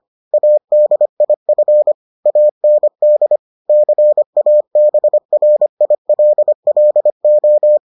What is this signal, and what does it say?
Morse code, heard as a keyed tone, about 25 wpm, 600 Hz. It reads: ADIF AND CABRILLO